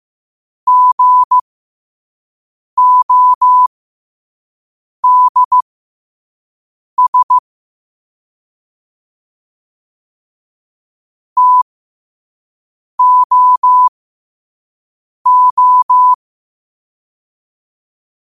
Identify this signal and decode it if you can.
Morse code, heard as a keyed tone, about 15 wpm, spaced as for 5 wpm, 1000 Hz. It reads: GODS TOO
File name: qotdfiles / 5wpm